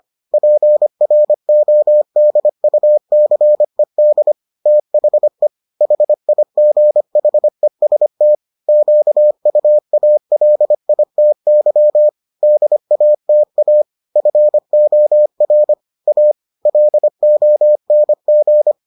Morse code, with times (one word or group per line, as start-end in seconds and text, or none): 0.3-4.3 PRODUCED
4.7-5.5 THE
5.8-8.3 HIGHEST
8.7-12.1 QUALITY
12.4-13.8 DATA
14.2-15.7 FOR
16.1-16.3 A
16.7-18.7 LONG